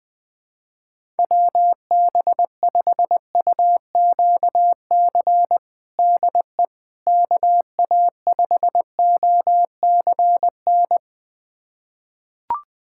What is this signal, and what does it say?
Morse code, heard as a keyed tone, about 20 wpm, 700 Hz. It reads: WB5UQC DE KA5OCN
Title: USA Callsign DE Callsign - 6 Characters - 20wpm